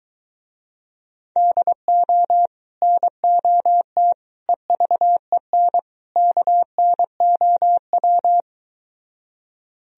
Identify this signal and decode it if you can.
Morse code, heard as a keyed tone, about 22 wpm, 700 Hz. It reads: DO NOT EVEN KNOW